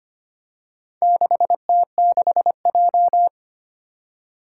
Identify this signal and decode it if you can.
Morse code, heard as a keyed tone, about 25 wpm, 700 Hz. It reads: 6T6J
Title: Four Challenging Letter-Number Combinations 25wpm